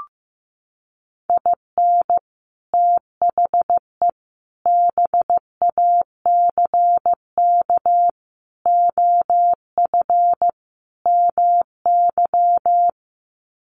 Morse code, heard as a keyed tone, about 15 wpm, 700 Hz. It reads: IN THE BACK OF MY
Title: Sets of 5 Words Encore 15wpm